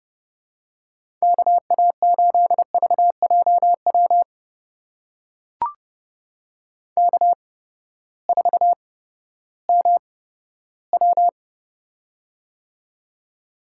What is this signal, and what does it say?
Morse code, heard as a keyed tone, about 30 wpm, spaced as for 15 wpm, 700 Hz. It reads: KA8VJW K4MW
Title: USA Call Signs - 8x Character Spacing - 30wpm